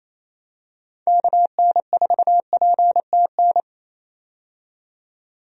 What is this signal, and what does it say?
Morse code, heard as a keyed tone, about 28 wpm, 700 Hz. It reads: KN4PTN
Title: USA Call Signs - 6 Characters - 28wpm